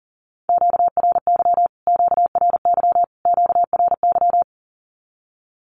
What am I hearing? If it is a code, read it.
Morse code, heard as a keyed tone, about 40 wpm, 700 Hz. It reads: QRY QRY QRY
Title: Q-Codes Repeated 3x 40wpm